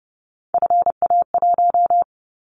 Morse code, heard as a keyed tone, about 30 wpm, 700 Hz. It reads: FA1
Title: Callsign Prefixes for Top 50 DXCC - 30wpm